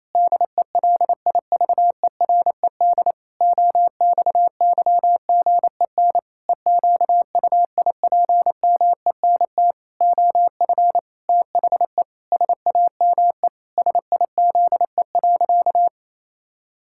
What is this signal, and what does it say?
Morse code, heard as a keyed tone, about 28 wpm, 700 Hz. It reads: DELIVERED OXYGEN EQUIPMENT OF THE SAME SIZE.
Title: Pangram Challenge v2 - 28wpm